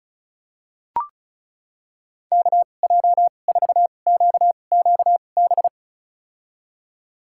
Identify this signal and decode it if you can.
Morse code, heard as a keyed tone, about 34 wpm, spaced as for 25 wpm, 700 Hz. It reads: KJ4QQB
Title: USA Call Signs - 6 Characters - 2x Character Spacing - 35wpm